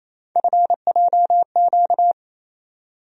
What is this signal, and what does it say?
Morse code, heard as a keyed tone, about 28 wpm, 700 Hz. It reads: FJQ
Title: USA Call Sign Postfixes - 3 Letters 28wpm